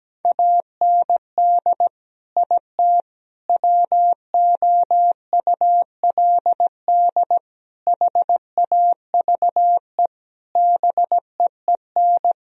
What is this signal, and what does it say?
Morse code, heard as a keyed tone, about 17 wpm, 700 Hz. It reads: AND IT WOULD HAVE BEEN
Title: Sentences from Top 1000 Words - Review Part 4 - 17wpm